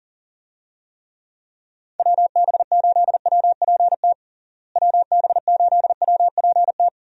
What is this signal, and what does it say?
Morse code, heard as a keyed tone, about 40 wpm, 700 Hz. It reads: WB8WPT WB8WPT
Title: USA Call Signs - 6 Characters - Repeated 3x - 40wpm